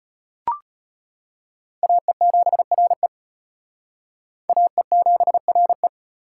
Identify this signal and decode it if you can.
Morse code, heard as a keyed tone, about 31 wpm, 700 Hz. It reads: AE7RE AE7RE
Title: Speed Racing - USA 2x2 Call Signs 25wpm